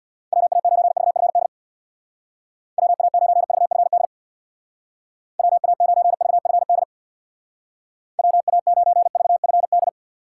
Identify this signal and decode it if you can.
Morse code, heard as a keyed tone, about 62 wpm, 700 Hz. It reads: WA9VFD WA9VFD WA9VFD WA9VFD
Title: Speed Racing - USA 2x3 Call Signs 50wpm